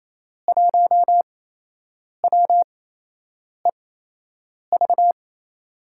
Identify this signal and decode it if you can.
Morse code, heard as a keyed tone, about 28 wpm, spaced as for 8 wpm, 700 Hz. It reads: 1WEV
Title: USA Call Signs - 8x Character Spacing - 28wpm